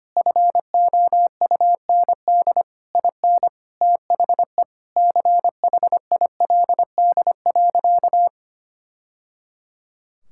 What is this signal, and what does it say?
Morse code, heard as a keyed tone, about 25 wpm, 700 Hz. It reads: FOUND IN THE CHILD.
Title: Sentences from Top 1000 Words - Review Part 5 - 25wpm